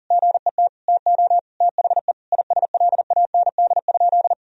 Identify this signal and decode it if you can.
Morse code, heard as a keyed tone, about 40 wpm, 700 Hz. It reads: GET TO THE ISLAND?
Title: Sentences from Top 1000 Words - Review Part 3 - 40wpm